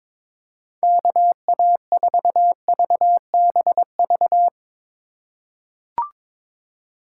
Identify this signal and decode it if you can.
Morse code, heard as a keyed tone, about 22 wpm, 700 Hz. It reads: KA4VBV
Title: USA Call Signs - 3x Character Spacing - 22wpm